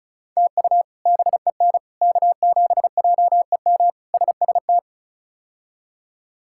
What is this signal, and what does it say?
Morse code, heard as a keyed tone, about 36 wpm, 700 Hz. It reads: TU BEN K7JEM SST
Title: K1USN SST QSO Practice - 35wpm